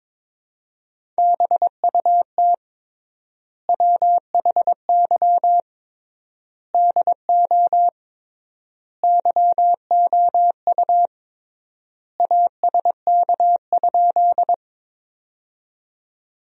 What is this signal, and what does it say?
Morse code, heard as a keyed tone, about 22 wpm, 700 Hz. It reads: BUT WHY DO YOU ASK?
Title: 3x Word-Spacing - Sentences from Top 200 Words - 22wpm